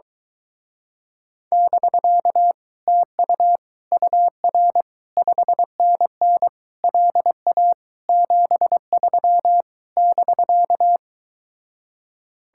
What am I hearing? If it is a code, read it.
Morse code, heard as a keyed tone, about 23 wpm, 700 Hz. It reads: <BK> TU UR 5NN LA 73 <BK>